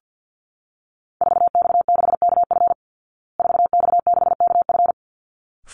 Morse code, heard as a keyed tone, about 48 wpm, 700 Hz. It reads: 4X6KF 4X6KF
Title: Call Signs Repeated 3x - 50wpm